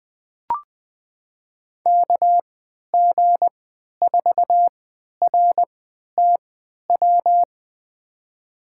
Morse code, heard as a keyed tone, about 20 wpm, spaced as for 11 wpm, 700 Hz. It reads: KG4RTW
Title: USA Call Signs - 3x Character Spacing - 20wpm